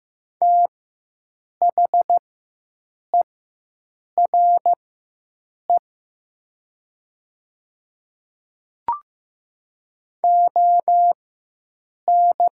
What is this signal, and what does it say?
Morse code, heard as a keyed tone, about 15 wpm, spaced as for 7 wpm, 700 Hz. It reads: THERE ON
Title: 4x Character Spacing - Top 100 Words - 15wpm